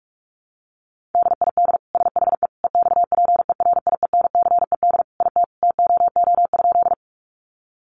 Morse code, heard as a keyed tone, about 46 wpm, 700 Hz. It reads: DID SHE EXPERIENCED IT TOO?
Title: Sentences from Top 1100 Words 45wpm